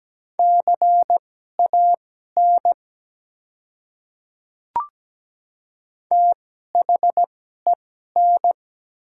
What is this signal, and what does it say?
Morse code, heard as a keyed tone, about 17 wpm, spaced as for 12 wpm, 700 Hz. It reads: CAN THEN
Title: 2x Character Spacing - Top 100 Words - 17wpm